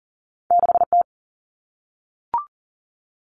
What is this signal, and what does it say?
Morse code, heard as a keyed tone, about 40 wpm, 700 Hz. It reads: 6T
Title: All Cut Numbers - 2-Digit 40wpm